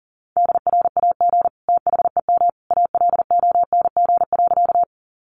Morse code, heard as a keyed tone, about 40 wpm, 700 Hz. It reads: DRAG THEM ALONG.
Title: Sentences from Top 1500 Words 40wpm